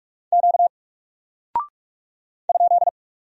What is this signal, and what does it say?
Morse code, heard as a keyed tone, about 44 wpm, 700 Hz. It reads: Q ?
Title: Single Character (Letter:Number:Punctuation:Prosign) 45wpm